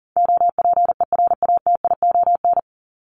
Morse code, heard as a keyed tone, about 40 wpm, 700 Hz. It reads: OPERATION
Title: Top 1501-1750 Words 40wpm